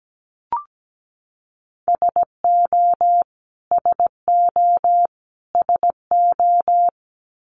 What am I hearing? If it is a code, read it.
Morse code, heard as a keyed tone, about 17 wpm, 700 Hz. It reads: SO SO SO
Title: Top 500 Words Repeated 17wpm